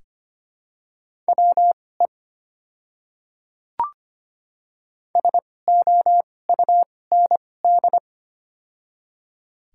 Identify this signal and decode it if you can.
Morse code, heard as a keyed tone, about 25 wpm, spaced as for 18 wpm, 700 Hz. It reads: WE SOUND
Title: 2x Character Spacing - Top 500 Words - 25wpm